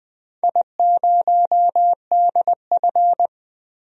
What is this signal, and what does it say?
Morse code, heard as a keyed tone, about 20 wpm, 700 Hz. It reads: I0DF
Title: DXCC-100 Call Signs - 4 Characters - 20wpm